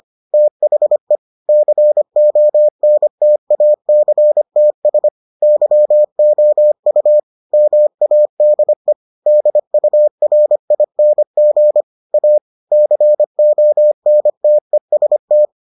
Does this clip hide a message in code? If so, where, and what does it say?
Yes: Morse code from 0.3 to 15.5 s, heard as a keyed tone, about 25 wpm, 600 Hz. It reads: THE CONTACTS YOU MADE DURING A CONTEST